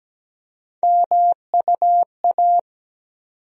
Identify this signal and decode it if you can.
Morse code, heard as a keyed tone, about 17 wpm, 700 Hz. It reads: MUA
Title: USA Call Sign Postfixes - 3 Letters 17wpm